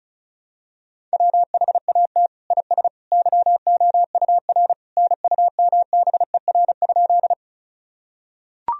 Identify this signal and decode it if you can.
Morse code, heard as a keyed tone, about 36 wpm, 700 Hz. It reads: WHAT IS YOUR NUMBER?
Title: Sentences from Top 1000 Words - Review Part 1 - 35wpm